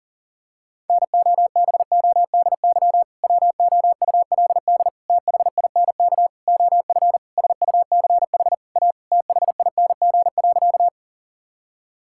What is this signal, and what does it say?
Morse code, heard as a keyed tone, about 40 wpm, 700 Hz. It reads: NOBODY WOULD THINK OF SUCH A THING.
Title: Sentences from Top 1600 Words 40wpm